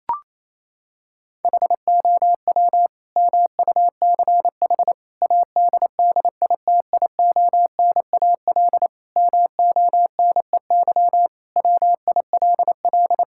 Morse code, heard as a keyed tone, about 28 wpm, 700 Hz. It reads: HOW MUCH ADDITIONAL MONEY WILL
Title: Sentences from Top 1000 Words - Review Part 5 - 28wpm